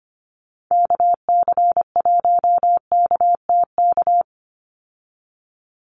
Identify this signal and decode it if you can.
Morse code, heard as a keyed tone, about 25 wpm, 700 Hz. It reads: KC1KTK
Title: USA Call Signs 25wpm